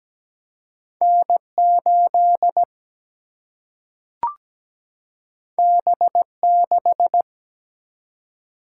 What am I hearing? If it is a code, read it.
Morse code, heard as a keyed tone, about 17 wpm, 700 Hz. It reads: N8 B6
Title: Single-Letter Single-Number 17wpm